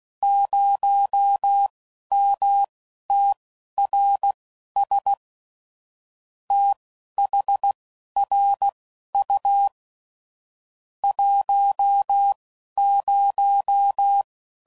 Morse code, heard as a keyed tone, about 16 wpm, 800 Hz. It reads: 0 M T R S T H R U 1 0
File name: CW Code Groups @15WPM / QSOs / WD0FBT Spaced